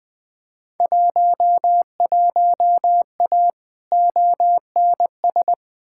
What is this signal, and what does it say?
Morse code, heard as a keyed tone, about 20 wpm, 700 Hz. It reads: 11A ONS
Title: Field Day Exchanges Encore - 20wpm